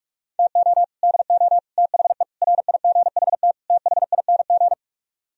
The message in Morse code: TO DO THE RIGHT THING

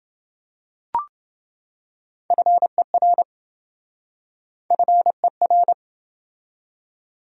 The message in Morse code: FER FER

